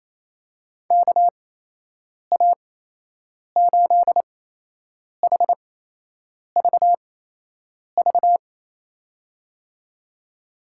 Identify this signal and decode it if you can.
Morse code, heard as a keyed tone, about 28 wpm, spaced as for 8 wpm, 700 Hz. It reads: KA8HVV
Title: USA Call Signs - 6 Characters - 8x Character Spacing - 28wpm